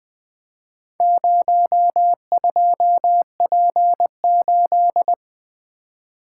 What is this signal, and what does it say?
Morse code, heard as a keyed tone, about 20 wpm, 700 Hz. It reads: 02P8
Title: Four Letter-Number Combinations - 20wpm